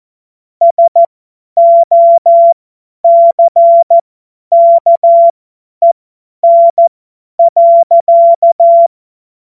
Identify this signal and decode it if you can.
Morse code, heard as a keyed tone, about 14 wpm, spaced as for 10 wpm, 680 Hz. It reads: SOCKEN.
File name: Morsekurs / Witze / Witz 09 April 2024 50 F70